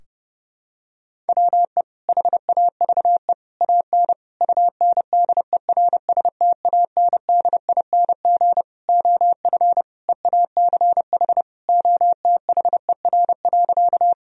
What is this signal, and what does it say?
Morse code, heard as a keyed tone, about 30 wpm, 700 Hz. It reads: WE HAVE AN UNDERSTANDING OF EACH OTHER.
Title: Sentences from Top 1000 Words - Review Part 3 - 30wpm